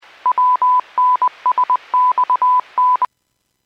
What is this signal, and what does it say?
Morse code, heard as a keyed tone, about 20 wpm, 1000 Hz. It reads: WNSXN
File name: WPS / 02878 04-40 WPS 451.25 WPS